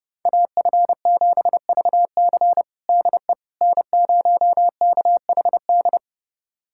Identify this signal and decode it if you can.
Morse code, heard as a keyed tone, about 30 wpm, 700 Hz. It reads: AF7VC DE N0KHD
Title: USA Callsign DE Callsign - 5 Characters - 30wpm